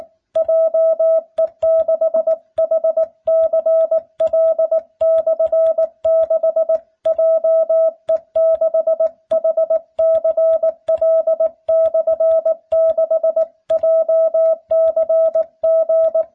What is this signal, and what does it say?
Morse code, heard as a keyed tone, about 19 wpm, 650 Hz. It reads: JE6HCL/6 JE6HCL/6 JCG